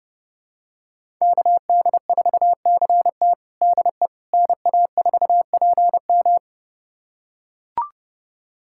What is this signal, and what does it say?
Morse code, heard as a keyed tone, about 32 wpm, 700 Hz. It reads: KD4CT DE NA4PM
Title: USA Callsign DE Callsign - 5 Characters - 30wpm